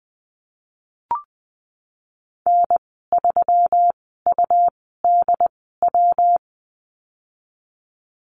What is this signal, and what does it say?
Morse code, heard as a keyed tone, about 20 wpm, spaced as for 15 wpm, 700 Hz. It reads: N3UDW
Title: 122 Lesson 20g - Number 3 - Call Signs 20wpm